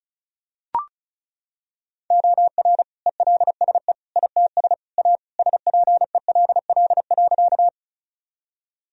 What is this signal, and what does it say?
Morse code, heard as a keyed tone, about 34 wpm, 700 Hz. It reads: OR ELSE ITS A SPELL.
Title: Sentences from Top 1000 Words - Review Part 1 - 35wpm